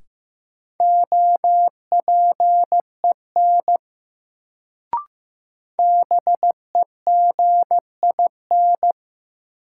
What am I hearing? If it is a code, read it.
Morse code, heard as a keyed tone, about 15 wpm, 700 Hz. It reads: OPEN BEGIN
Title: Sentences from Top 300 Words - New Word Drill - 15wpm